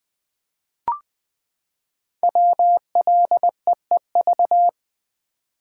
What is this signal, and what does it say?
Morse code, heard as a keyed tone, about 20 wpm, 700 Hz. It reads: WLEEV
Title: Code Groups - 5 Letters - 20wpm